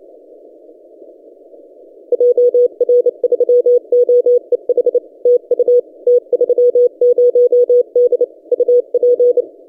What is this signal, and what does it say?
Morse code, heard as a keyed tone, about 27 wpm, 490 Hz. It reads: JR3OEH TU T30D UP
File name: t30d-12cw